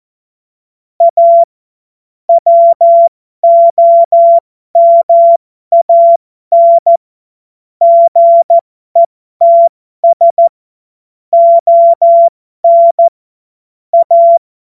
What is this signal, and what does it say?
Morse code, heard as a keyed tone, about 14 wpm, spaced as for 12 wpm, 680 Hz. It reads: A WOMAN GETS ON A
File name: Morsekurs / Witze